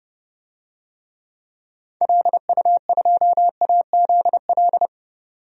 Morse code, heard as a keyed tone, about 30 wpm, 700 Hz. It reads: LU2AZL
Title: DXCC-100 Call Signs - 6 Characters - 30wpm